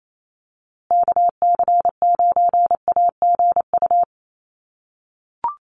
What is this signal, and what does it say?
Morse code, heard as a keyed tone, about 28 wpm, 700 Hz. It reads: KC9AGU
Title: USA Call Signs Repeated 2x 28wpm